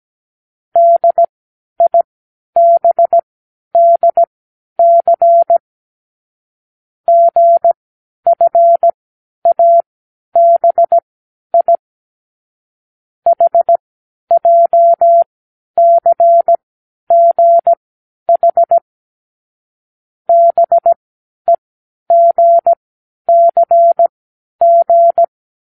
Morse code, heard as a keyed tone, about 17 wpm, spaced as for 10 wpm, 680 Hz. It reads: DIBDC GFABI HJCGH BEGCG
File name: k7qo / Lesson15(A-J)